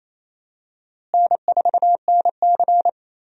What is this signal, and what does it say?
Morse code, heard as a keyed tone, about 28 wpm, 700 Hz. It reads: N4NC
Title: USA Call Signs - 4 Characters - Repeated 3x - 28wpm